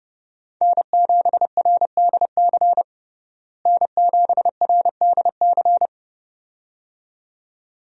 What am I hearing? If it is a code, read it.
Morse code, heard as a keyed tone, about 30 wpm, 700 Hz. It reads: N7RDC N7RDC
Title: USA Call Signs - 5 Characters - Repeated 2x - 30wpm